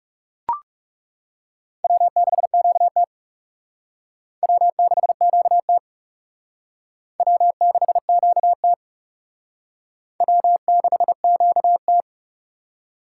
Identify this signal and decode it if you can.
Morse code, heard as a keyed tone, about 37 wpm, 700 Hz. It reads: W6QT W6QT W6QT W6QT
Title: Speed Racing - USA Call Signs 30wpm